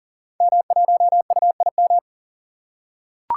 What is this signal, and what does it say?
Morse code, heard as a keyed tone, about 38 wpm, 700 Hz. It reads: M1UIM
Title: Code Groups - 5 Letter:Numbers - 40wpm